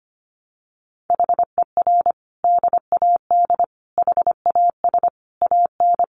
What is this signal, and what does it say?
Morse code, heard as a keyed tone, about 25 wpm, 700 Hz. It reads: HER DAD HAS AN
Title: Additional Easy Phrases 25wpm